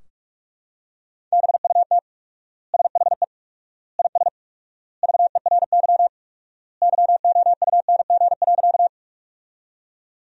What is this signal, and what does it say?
Morse code, heard as a keyed tone, about 42 wpm, 700 Hz. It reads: BUT SHE IS VERY YOUNG.
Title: 4x Word-Spacing - Sentences from Top 400 Words - 45wpm